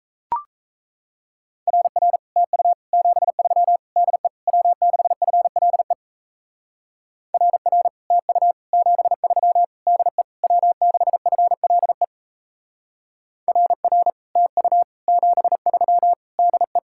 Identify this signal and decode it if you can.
Morse code, heard as a keyed tone, about 34 wpm, 700 Hz. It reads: RR TU 73 DE W6FLE RR TU 73 DE W6FLE RR TU 73 DE
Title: Speed-Racing Common QSO Phrases - 28wpm